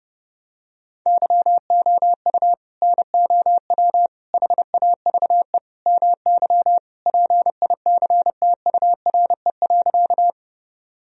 Morse code, heard as a keyed tone, about 30 wpm, 700 Hz. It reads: YOU NOW HAVE MY PICTURE.
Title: Sentences from Top 1000 Words - Review Part 1 - 30wpm